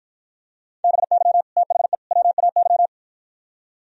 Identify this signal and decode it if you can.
Morse code, heard as a keyed tone, about 52 wpm, 700 Hz. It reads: BY THE WAY